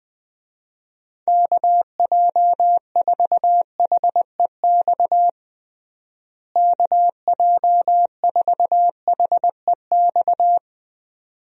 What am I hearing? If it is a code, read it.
Morse code, heard as a keyed tone, about 20 wpm, 700 Hz. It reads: KJ4HEX KJ4HEX
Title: USA Call Signs - 6 Characters - Repeated 2x - 20wpm